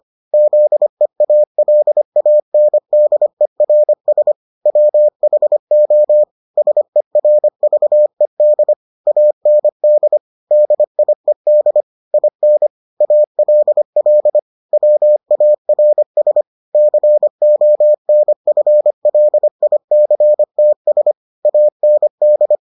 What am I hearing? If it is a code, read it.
Morse code, heard as a keyed tone, about 25 wpm, 600 Hz. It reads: ZEALANDERS WHO SERVED AND DIED IN ALL WARS CONFLICTS AND